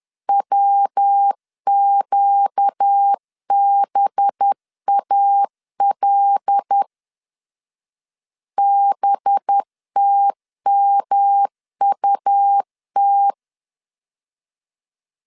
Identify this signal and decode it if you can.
Morse code, heard as a keyed tone, about 10 wpm, 800 Hz. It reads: WQBAL BTMUT
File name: CW / DET / 1 6 1500C S B 10wpm 4137